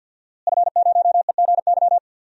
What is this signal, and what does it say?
Morse code, heard as a keyed tone, about 47 wpm, 700 Hz. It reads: F0EGY
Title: DXCC-100 Call Signs - 5 Characters - Repeated 2x - 50wpm